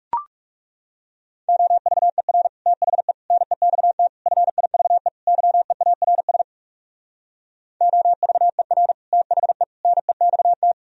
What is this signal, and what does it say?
Morse code, heard as a keyed tone, about 38 wpm, 700 Hz. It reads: OVER THE NEXT FIVE YEARS OVER THE NEXT